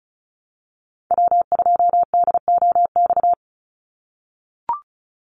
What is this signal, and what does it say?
Morse code, heard as a keyed tone, about 34 wpm, 700 Hz. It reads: W2DOX
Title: USA Call Signs 35wpm